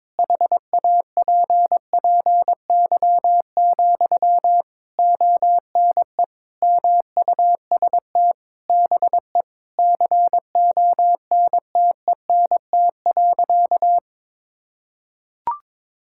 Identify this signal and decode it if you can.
Morse code, heard as a keyed tone, about 22 wpm, 700 Hz. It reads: HAPPY, ONE MUST BE CONTENT.
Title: Sentences from Top 1000 Words - Review Part 6 - 22wpm